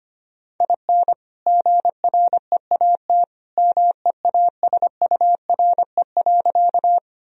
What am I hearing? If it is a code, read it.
Morse code, heard as a keyed tone, about 25 wpm, 700 Hz. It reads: IN GREAT MEASURE.
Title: Sentences from Top 1000 Words - Review Part 2 - 25wpm